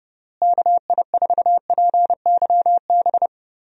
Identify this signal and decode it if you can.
Morse code, heard as a keyed tone, about 32 wpm, 700 Hz. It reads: KI4PYB
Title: USA Call Signs - 6 Characters - Repeated 3x - 30wpm